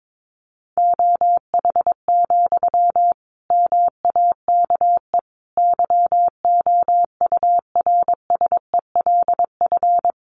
Morse code, heard as a keyed tone, about 22 wpm, 700 Hz. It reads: OH, MAKE YOURSELF
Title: Sentences from Top 700 Words 22wpm